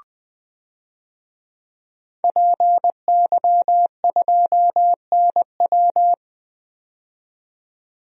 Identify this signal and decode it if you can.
Morse code, heard as a keyed tone, about 21 wpm, 700 Hz. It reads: PY2NW
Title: DXCC-100 Call Signs - 5 Characters - 20wpm